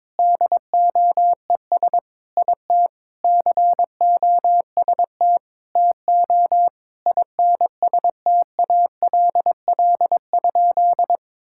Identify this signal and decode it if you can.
Morse code, heard as a keyed tone, about 22 wpm, 700 Hz. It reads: DOES IT COST TO INSTALL?
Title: Sentences from Top 1600 Words 22wpm